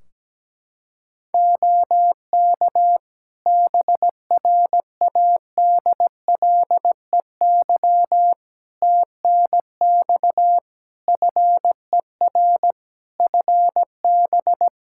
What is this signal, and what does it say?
Morse code, heard as a keyed tone, about 17 wpm, 700 Hz. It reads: OK BRADLEY TNX FER FB
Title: Practice QSO - Exchange 4 - Ending - 17wpm